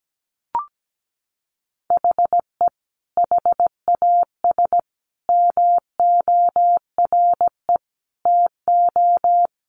Morse code, heard as a keyed tone, about 17 wpm, 700 Hz. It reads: HE HAS MORE TO
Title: Sentences from Top 200 Words 17wpm